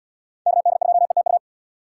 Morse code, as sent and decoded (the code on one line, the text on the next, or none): -.- -.. ..--- . - ..-
KD2ETU